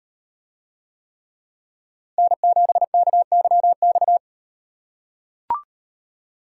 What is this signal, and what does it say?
Morse code, heard as a keyed tone, about 35 wpm, 700 Hz. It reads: N7KYX